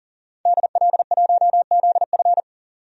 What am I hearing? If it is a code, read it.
Morse code, heard as a keyed tone, about 40 wpm, 700 Hz. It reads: DL1ZF
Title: DXCC-100 Call Signs - 5 Characters - 40wpm